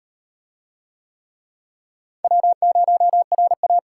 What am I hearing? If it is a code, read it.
Morse code, heard as a keyed tone, about 38 wpm, 700 Hz. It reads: W0RA